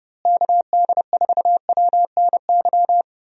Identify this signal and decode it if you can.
Morse code, heard as a keyed tone, about 30 wpm, 700 Hz. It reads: KD4WNY
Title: USA Call Signs - 6 Characters - Repeated 2x - 30wpm